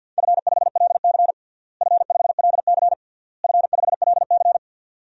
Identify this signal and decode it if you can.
Morse code, heard as a keyed tone, about 49 wpm, 700 Hz. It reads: F5LC F5LC F5LC
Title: DXCC-100 Call Signs - 4 Characters - Repeated 3x - 50wpm